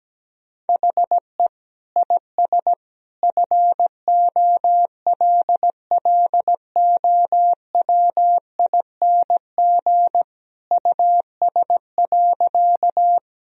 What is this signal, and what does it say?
Morse code, heard as a keyed tone, about 17 wpm, 700 Hz. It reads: HE IS FOLLOWING US.
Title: Sentences from Top 1000 Words - Review Part 4 - 17wpm